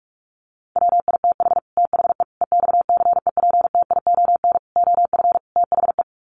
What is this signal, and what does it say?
Morse code, heard as a keyed tone, about 44 wpm, 700 Hz. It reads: WITH THE EXCEPTION OF THE
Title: Sets of 5 Words Encore 45wpm